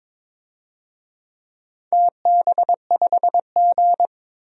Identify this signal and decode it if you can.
Morse code, heard as a keyed tone, about 23 wpm, 700 Hz. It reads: TB5G
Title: DXCC-100 Call Signs - 4 Characters - 22wpm